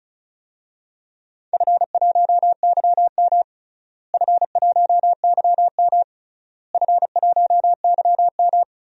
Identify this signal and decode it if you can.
Morse code, heard as a keyed tone, about 36 wpm, 700 Hz. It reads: F1YM F1YM F1YM
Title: DXCC-100 Call Signs - 4 Characters - Repeated 3x - 35wpm